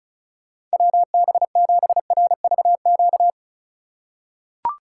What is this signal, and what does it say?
Morse code, heard as a keyed tone, about 36 wpm, 700 Hz. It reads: WB7RVQ
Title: USA Call Signs - 6 Characters - 3x Character Spacing - 35wpm